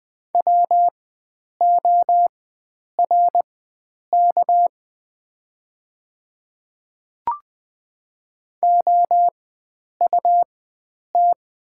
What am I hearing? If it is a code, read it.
Morse code, heard as a keyed tone, about 20 wpm, spaced as for 9 wpm, 700 Hz. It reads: WORK OUT
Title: 4x Character Spacing - Top 300 Words - 20wpm